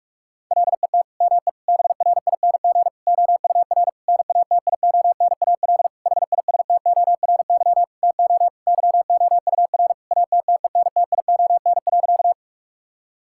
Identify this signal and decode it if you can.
Morse code, heard as a keyed tone, about 46 wpm, 700 Hz. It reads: LET ME BRING OUR NATIONAL HISTORY TO YOUR ATTENTION.
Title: Sentences from Top 1000 Words - Review Part 6 - 45wpm